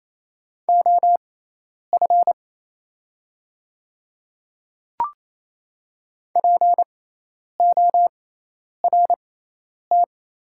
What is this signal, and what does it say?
Morse code, heard as a keyed tone, about 28 wpm, spaced as for 10 wpm, 690 Hz. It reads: OF PORT
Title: 6x Character Spacing - Top 200 Words - 28wpm